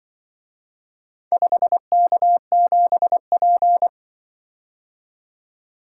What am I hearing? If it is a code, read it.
Morse code, heard as a keyed tone, about 24 wpm, 700 Hz. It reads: 5K7P
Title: Speed-Racing - Four Letter-Number Combinations - 20wpm